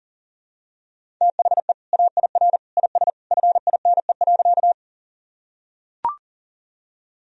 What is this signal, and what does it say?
Morse code, heard as a keyed tone, about 40 wpm, 700 Hz. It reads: THE AIR IS FINE.
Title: Sentences from Top 1000 Words - Review Part 1 - 40wpm